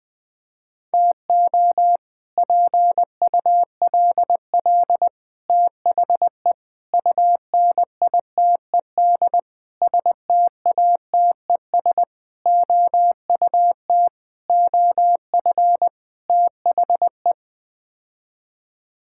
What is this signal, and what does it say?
Morse code, heard as a keyed tone, about 20 wpm, 700 Hz. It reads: TO PULL THE UNITED STATES OUT OF THE
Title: Sets of 8 Words - 20wpm